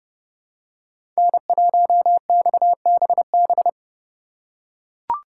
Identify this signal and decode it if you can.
Morse code, heard as a keyed tone, about 30 wpm, 700 Hz. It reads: N1XBB
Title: USA Call Signs - 5 Characters - 2x Character Spacing - 30wpm